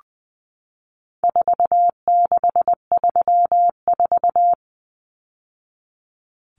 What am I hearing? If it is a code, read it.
Morse code, heard as a keyed tone, about 20 wpm, 700 Hz. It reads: 4634
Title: Numbers - 4-Digit 20wpm